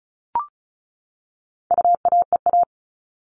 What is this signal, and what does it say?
Morse code, heard as a keyed tone, about 34 wpm, 700 Hz. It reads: UAEA
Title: All Cut Numbers - 4-Digit 35wpm